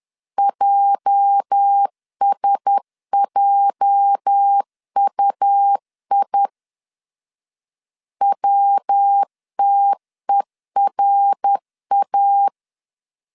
Morse code, heard as a keyed tone, about 10 wpm, 800 Hz. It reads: JSJUI WTERA
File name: LEZIONI / 8 YJ45 / 8 YJ 10wpm 2651